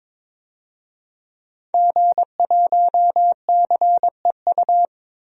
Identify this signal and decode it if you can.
Morse code, heard as a keyed tone, about 22 wpm, 700 Hz. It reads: G1CEU
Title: DXCC-100 Call Signs - 5 Characters - 22wpm